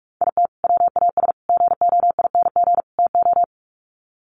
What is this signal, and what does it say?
Morse code, heard as a keyed tone, about 44 wpm, 700 Hz. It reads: IT WAS GOING TO
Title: Sets of 4 Words 45wpm